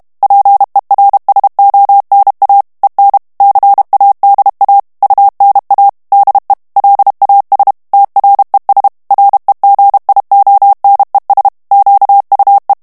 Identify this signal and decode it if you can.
Morse code, heard as a keyed tone, about 32 wpm, 800 Hz. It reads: PERSONA EN CADA UNA DE LAS TRES REGIONES QUE
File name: morse mp3 / ESPAGNOL TEXTE EN CLAIR 1 VITESSE 30 MOTS MINUTES